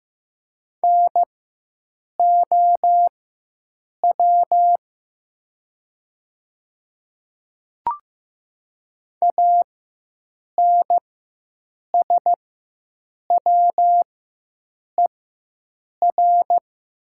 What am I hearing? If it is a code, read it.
Morse code, heard as a keyed tone, about 15 wpm, spaced as for 7 wpm, 690 Hz. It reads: NOW ANSWER
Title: 4x Character Spacing - Top 500 Words - 15wpm